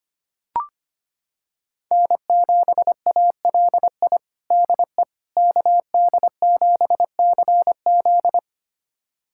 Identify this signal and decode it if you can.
Morse code, heard as a keyed tone, about 25 wpm, 700 Hz. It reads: N7ALI DE KD7CZ
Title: USA Callsign DE Callsign - 5 Characters - 25wpm